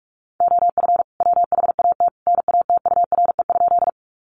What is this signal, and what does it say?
Morse code, heard as a keyed tone, about 44 wpm, 700 Hz. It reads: OF WHAT NATURE?
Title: Sentences from Top 600 Words 45wpm